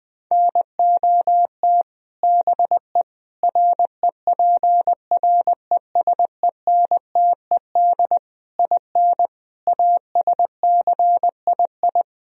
Morse code, heard as a keyed tone, about 20 wpm, 700 Hz. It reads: NOT BE REPRESENTED IN ASCII
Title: Speed-Racing - Sets of 5 Words 17wpm